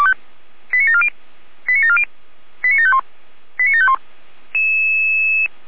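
Five-tone selective call (ZVEI), heard as quick runs of five tones, tones of about 70 ms. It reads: 89740 89740 89751 89751
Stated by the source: ZVEI five-tone selective call 89590 received by SDR.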